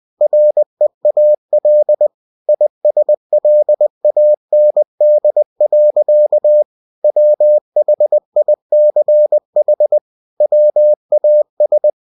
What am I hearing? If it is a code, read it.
Morse code, heard as a keyed tone, about 20 wpm, 600 Hz. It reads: REAL ISLAND. WHICH WAS
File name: cw-books / winnie-the-pooh / winnie-20wpm-page-0096